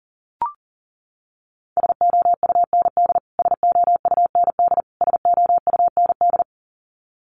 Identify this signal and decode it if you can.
Morse code, heard as a keyed tone, about 40 wpm, 700 Hz. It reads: SOUND SOUND SOUND
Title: Top 500 Words Repeated 40wpm